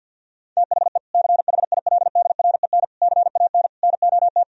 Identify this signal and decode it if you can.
Morse code, heard as a keyed tone, about 46 wpm, 700 Hz. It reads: THE CHILDREN CAN NOT
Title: Sentences from Top 1000 Words - Review Part 4 - 50wpm